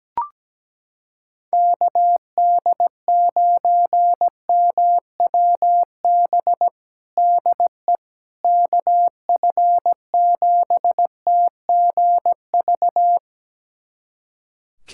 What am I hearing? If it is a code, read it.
Morse code, heard as a keyed tone, about 17 wpm, 700 Hz. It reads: KD9MWB DE KF7TGV
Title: USA Callsign DE Callsign - 6 Characters - 17wpm